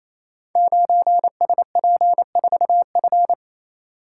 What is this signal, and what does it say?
Morse code, heard as a keyed tone, about 29 wpm, 700 Hz. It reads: 9SP4F
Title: Code Groups - 5 Letter:Numbers - 28wpm